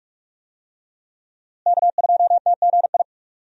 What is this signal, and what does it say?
Morse code, heard as a keyed tone, about 45 wpm, 700 Hz. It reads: K2TGI
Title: USA Call Signs - 5 Characters - Repeated 3x - 45wpm